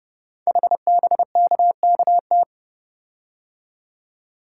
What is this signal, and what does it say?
Morse code, heard as a keyed tone, about 30 wpm, 700 Hz. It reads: HBKKT
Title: Code Groups - 5 Letters - 30wpm